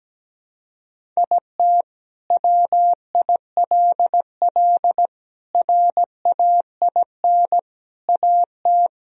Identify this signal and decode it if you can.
Morse code, heard as a keyed tone, about 17 wpm, 700 Hz. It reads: IT WILL RAIN AT
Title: Sentences from Top 1000 Words - Review Part 1 - 17wpm